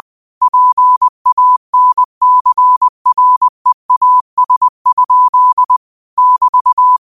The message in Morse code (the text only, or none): PANCREAS? <BT>